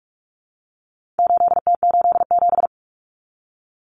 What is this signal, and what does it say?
Morse code, heard as a keyed tone, about 44 wpm, 700 Hz. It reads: 8T87
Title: Common Cut Numbers - 4-Digit 45wpm